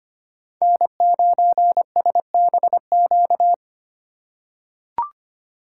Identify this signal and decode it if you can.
Morse code, heard as a keyed tone, about 25 wpm, 700 Hz. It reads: N9SBQ
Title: USA Call Signs - 5 Characters - 3x Character Spacing - 25wpm